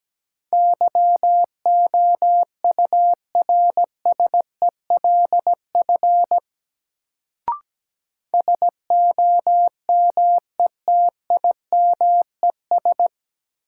Morse code, heard as a keyed tone, about 17 wpm, 700 Hz. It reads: YOURSELF SOMETIMES
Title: Sentences from Top 600 Words - New Word Drill - 17wpm